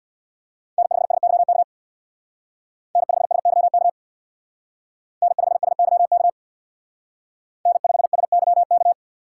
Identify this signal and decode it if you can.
Morse code, heard as a keyed tone, about 57 wpm, 700 Hz. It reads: N5SYK N5SYK N5SYK N5SYK
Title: Speed Racing - USA 1x3 Call Signs 50wpm